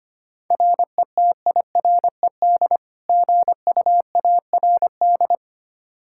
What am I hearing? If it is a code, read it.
Morse code, heard as a keyed tone, about 25 wpm, 700 Hz. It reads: RETIRED GUARD
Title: Practice QSO - Exchange 3 - More Info - 25wpm